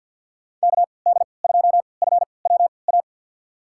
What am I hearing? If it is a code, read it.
Morse code, heard as a keyed tone, about 46 wpm, spaced as for 27 wpm, 700 Hz. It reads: KD2FWA
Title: USA Call Signs - 6 Characters - 3x Character Spacing - 50wpm